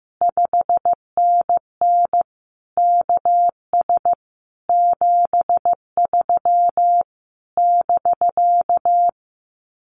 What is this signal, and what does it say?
Morse code, heard as a keyed tone, about 15 wpm, 700 Hz. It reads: 5NN KS 73 <BK>
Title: Common QSO Phrases Encore 15wpm